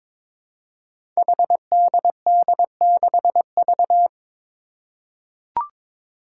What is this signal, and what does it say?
Morse code, heard as a keyed tone, about 22 wpm, 700 Hz. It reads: HDD6V
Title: Five Challenging Letter-Number Combinations 22wpm